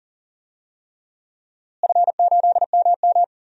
Morse code, heard as a keyed tone, about 40 wpm, 700 Hz. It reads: F8MM
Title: DXCC-100 Call Signs - 4 Characters - Repeated 3x - 40wpm